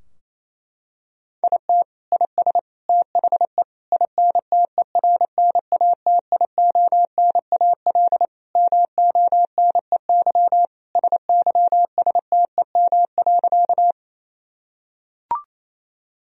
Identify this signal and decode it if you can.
Morse code, heard as a keyed tone, about 28 wpm, 700 Hz. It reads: IT IS THE INTERNATIONAL MONEY SYSTEM.
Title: Sentences from Top 1000 Words - Review Part 4 - 28wpm